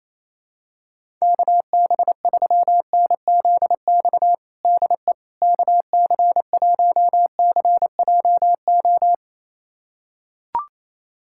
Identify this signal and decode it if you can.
Morse code, heard as a keyed tone, about 28 wpm, 700 Hz. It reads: KB3NZX DE KC1CJO
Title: USA Callsign DE Callsign 28wpm